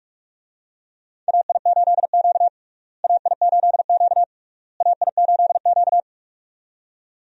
Morse code, heard as a keyed tone, about 46 wpm, 700 Hz. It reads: AI8Q AI8Q AI8Q
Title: USA Call Signs - 4 Characters - Repeated 3x - 45wpm